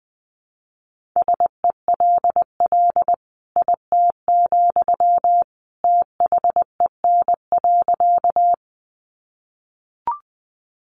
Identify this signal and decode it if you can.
Morse code, heard as a keyed tone, about 20 wpm, 700 Hz. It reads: SELL IT, THEN.
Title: Sentences from Top 1100 Words 20wpm